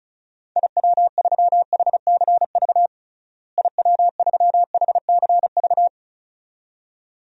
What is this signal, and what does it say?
Morse code, heard as a keyed tone, about 34 wpm, 700 Hz. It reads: IW3HCV IW3HCV
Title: DXCC-100 Call Signs - 6 Characters - Repeated 2x - 35wpm